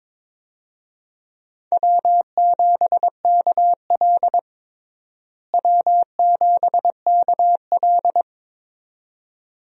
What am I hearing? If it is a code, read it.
Morse code, heard as a keyed tone, about 22 wpm, 700 Hz. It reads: W7KL W7KL
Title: USA Call Signs - 4 Characters - Repeated 2x - 22wpm